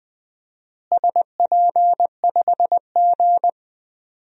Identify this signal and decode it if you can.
Morse code, heard as a keyed tone, about 20 wpm, 700 Hz. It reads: SP5G
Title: DXCC-100 Call Signs - 4 Characters - Repeated 3x - 20wpm